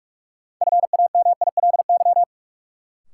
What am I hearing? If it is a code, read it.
Morse code, heard as a keyed tone, about 42 wpm, 700 Hz. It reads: FAMILY
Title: Sentences from Top 400 Words - New Word Drill - 45wpm